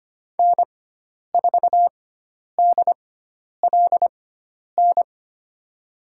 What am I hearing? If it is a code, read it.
Morse code, heard as a keyed tone, about 25 wpm, spaced as for 10 wpm, 700 Hz. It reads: N4DLN
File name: Lesson 16g - Letter c - Call Signs 25wpm